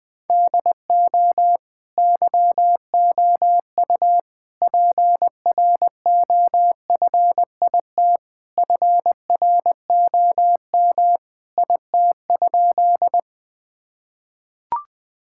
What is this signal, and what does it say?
Morse code, heard as a keyed tone, about 21 wpm, 700 Hz. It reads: DO YOU PROFIT FROM IT?
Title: Sentences from Top 1000 Words - Review Part 5 - 20wpm